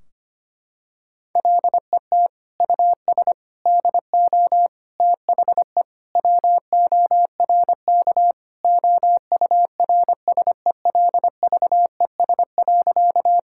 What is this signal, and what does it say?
Morse code, heard as a keyed tone, about 25 wpm, 700 Hz. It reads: LET US DO THE WORK OURSELVES.
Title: Sentences from Top 1600 Words 25wpm